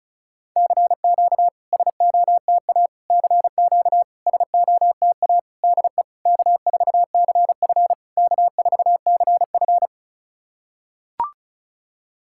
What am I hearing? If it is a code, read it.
Morse code, heard as a keyed tone, about 34 wpm, 690 Hz. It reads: CQ SOTA CQ SOTA DE K4CF K4CF
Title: SOTA QSO Practice 35wpm